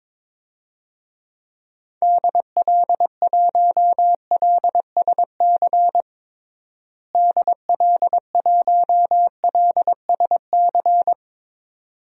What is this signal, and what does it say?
Morse code, heard as a keyed tone, about 22 wpm, 700 Hz. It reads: DL1LSC DL1LSC
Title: DXCC-100 Call Signs - 6 Characters - Repeated 2x - 22wpm